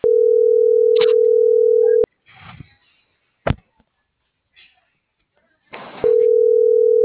Ringback tone (North America), 440 + 480 Hz.